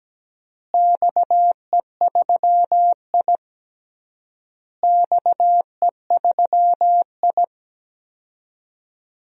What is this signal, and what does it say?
Morse code, heard as a keyed tone, about 17 wpm, 700 Hz. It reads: XE3I XE3I
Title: DXCC-100 Call Signs - 4 Characters - Repeated 3x - 17wpm